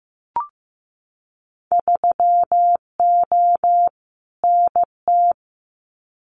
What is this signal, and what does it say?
Morse code, heard as a keyed tone, about 15 wpm, 700 Hz. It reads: 3O NT
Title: Winter Field Day Exchanges Encore 15wpm